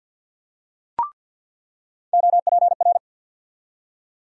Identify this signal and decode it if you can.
Morse code, heard as a keyed tone, about 47 wpm, 700 Hz. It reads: OPR
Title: USA Call Sign Postfixes - 3 Letters 50wpm